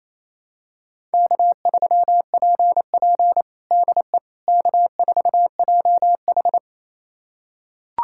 Morse code, heard as a keyed tone, about 28 wpm, 690 Hz. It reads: K3PP DE K4JH
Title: USA Callsign DE Callsign - 4 Characters - 28wpm